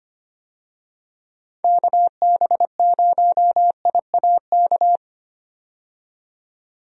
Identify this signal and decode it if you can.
Morse code, heard as a keyed tone, about 25 wpm, 700 Hz. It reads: KB0IAK
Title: USA Call Signs - 6 Characters - 25wpm